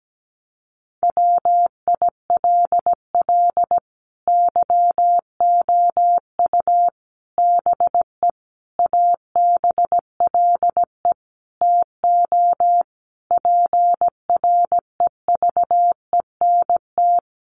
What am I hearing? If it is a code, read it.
Morse code, heard as a keyed tone, about 17 wpm, 700 Hz. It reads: WILL YOU BE ABLE TO PREVENT
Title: Sentences from Top 1100 Words 17wpm